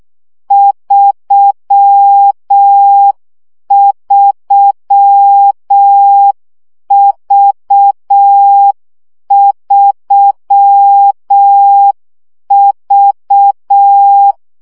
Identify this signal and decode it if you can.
Morse code, heard as a keyed tone, about 6 wpm, 800 Hz. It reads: 33V3V
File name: CW gyakorlas v 3 5 baud